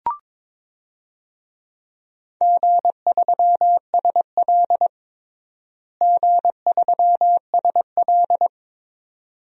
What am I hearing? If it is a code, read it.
Morse code, heard as a keyed tone, about 22 wpm, 700 Hz. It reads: G3SL G3SL
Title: DXCC-100 Call Signs - 4 Characters - Repeated 2x - 22wpm